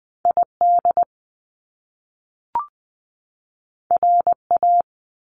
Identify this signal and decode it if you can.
Morse code, heard as a keyed tone, about 20 wpm, 700 Hz. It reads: ID RA
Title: Top Two Letter Combinations in Words 20wpm